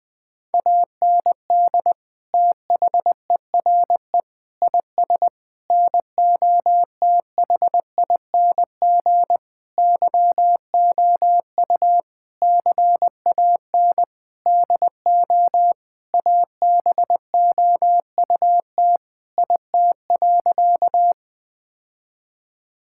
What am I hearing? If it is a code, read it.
Morse code, heard as a keyed tone, about 20 wpm, 700 Hz. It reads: AND THERE IS NOTHING YOU CAN DO ABOUT IT.
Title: Sentences with 8-Word Phrases - 20wpm